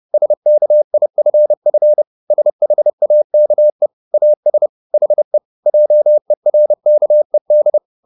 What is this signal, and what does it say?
Morse code, heard as a keyed tone, about 30 wpm, 600 Hz. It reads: SKIFF SHAKE AS HE JERKED